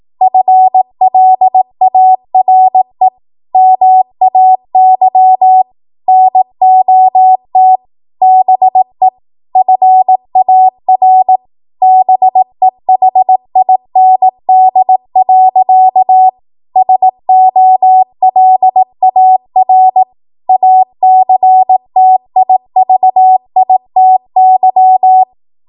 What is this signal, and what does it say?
Morse code, heard as a keyed tone, about 18 wpm, 750 Hz. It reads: FLARE MAY NOT BE FAR BEHIND. SOLAR ACTIVITY
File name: Morse / 18WPM